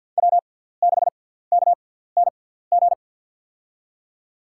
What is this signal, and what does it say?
Morse code, heard as a keyed tone, about 52 wpm, spaced as for 17 wpm, 700 Hz. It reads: W6KNG
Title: USA Call Signs - 5 Characters - 6x Character Spacing - 50wpm